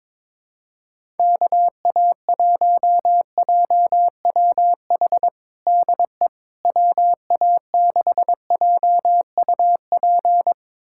Morse code, heard as a keyed tone, about 22 wpm, 700 Hz. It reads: KA1JWH DE WA6JUP
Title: USA Callsign DE Callsign - 6 Characters - 22wpm